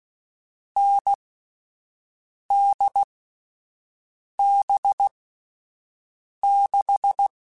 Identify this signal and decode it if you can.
Morse code, heard as a keyed tone, about 16 wpm, spaced as for 5 wpm, 800 Hz. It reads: NDB6
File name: Difficult Combos / Difficult Combos-mp3s / awj1 ndb6 gz7 @5